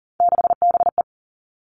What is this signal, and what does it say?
Morse code, heard as a keyed tone, about 41 wpm, 690 Hz. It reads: 6BE
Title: Three Challenging Letter-Number Combinations 40wpm